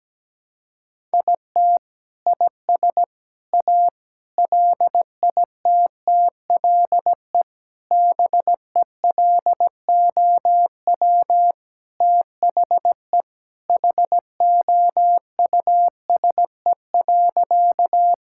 Morse code, heard as a keyed tone, about 17 wpm, 700 Hz. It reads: IT IS A LITTLE BELOW THE HOUSE.
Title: Sentences from Top 1000 Words - Review Part 5 - 17wpm